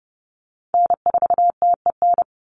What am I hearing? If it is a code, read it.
Morse code, heard as a keyed tone, about 30 wpm, 700 Hz. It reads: N4TEN
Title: USA Call Signs 30wpm